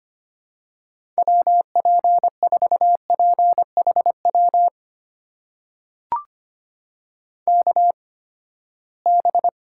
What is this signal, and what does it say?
Morse code, heard as a keyed tone, about 25 wpm, spaced as for 17 wpm, 700 Hz. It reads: WP4PHW KB
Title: USA Call Signs - 6 Characters - 8x Character Spacing - 25wpm